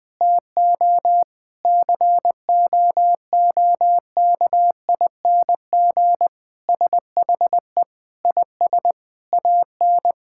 Morse code, heard as a keyed tone, about 21 wpm, 700 Hz. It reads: TO COOKING SHE IS AN
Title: Sentences with 5-Word Phrases - Spoken Beginning - 20wpm